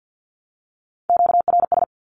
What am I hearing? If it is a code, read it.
Morse code, heard as a keyed tone, about 48 wpm, 700 Hz. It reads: QRS
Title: Q-Codes 50wpm